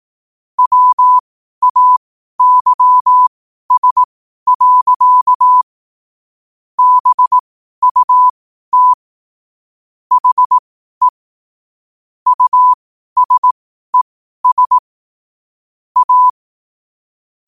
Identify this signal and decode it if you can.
Morse code, heard as a keyed tone, about 18 wpm, spaced as for 12 wpm, 1000 Hz. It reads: WAYS. BUT HE USES A
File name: qotdfiles / 12wpm